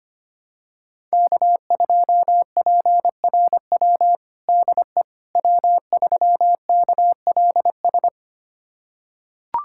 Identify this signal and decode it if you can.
Morse code, heard as a keyed tone, about 25 wpm, 700 Hz. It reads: K2PRW DE W3KLS
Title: USA Callsign DE Callsign - 5 Characters - 25wpm